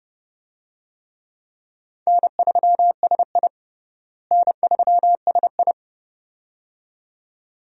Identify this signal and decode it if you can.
Morse code, heard as a keyed tone, about 30 wpm, 700 Hz. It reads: N3SI N3SI
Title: USA Call Signs - 4 Characters - Repeated 2x - 30wpm